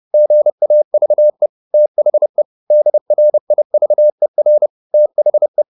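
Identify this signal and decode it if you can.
Morse code, heard as a keyed tone, about 30 wpm, 600 Hz. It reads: GAVE THE DRIVER THE
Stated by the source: Morse code at 30 wpm, the novel Never by Ken Follett in English, from CW med Gnister.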